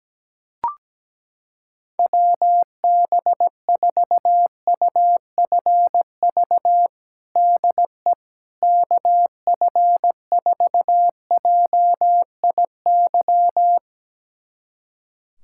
Morse code, heard as a keyed tone, about 17 wpm, 700 Hz. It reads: WB4UFV DE KF4JIY
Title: USA Callsign DE Callsign - 6 Characters - 17wpm